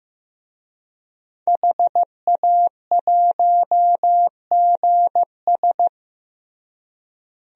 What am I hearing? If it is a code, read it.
Morse code, heard as a keyed tone, about 16 wpm, 700 Hz. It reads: HA1GS
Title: DXCC-100 Call Signs - 5 Characters - Repeated 2x - 15wpm